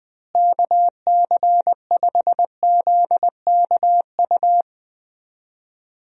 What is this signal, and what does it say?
Morse code, heard as a keyed tone, about 20 wpm, 700 Hz. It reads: KC5ZKU
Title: USA Call Signs - 6 Characters - Repeated 3x - 20wpm